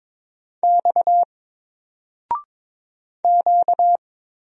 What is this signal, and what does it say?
Morse code, heard as a keyed tone, about 22 wpm, 700 Hz. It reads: X Q
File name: Single-Letter-Number-22wpm